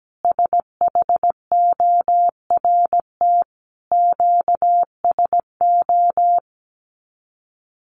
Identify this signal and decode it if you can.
Morse code, heard as a keyed tone, about 17 wpm, 700 Hz. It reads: SHORT QSO
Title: Common QSO Phrases 17wpm